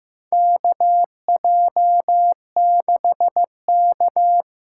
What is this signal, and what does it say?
Morse code, heard as a keyed tone, about 15 wpm, 700 Hz. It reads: KJ6K
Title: USA Call Signs - 6 Characters - 8x Character Spacing - 15wpm